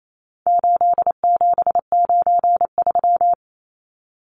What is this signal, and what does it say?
Morse code, heard as a keyed tone, about 28 wpm, 700 Hz. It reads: 8793
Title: Numbers - 4-Digit 28wpm